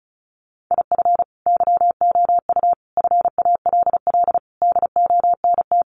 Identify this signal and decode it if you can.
Morse code, heard as a keyed tone, about 34 wpm, 700 Hz. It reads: IF YOU FALL DONT
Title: Sentences from Top 500 Words 35wpm